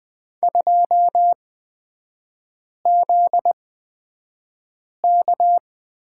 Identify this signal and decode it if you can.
Morse code, heard as a keyed tone, about 20 wpm, spaced as for 5 wpm, 700 Hz. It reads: 2ZK
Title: 243 Lesson 39e - Letter z - Three Characters 20wpm